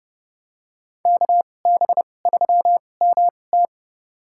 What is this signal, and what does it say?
Morse code, heard as a keyed tone, about 30 wpm, spaced as for 22 wpm, 700 Hz. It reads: KB3MT
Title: USA Call Signs - 5 Characters - 2x Character Spacing - 30wpm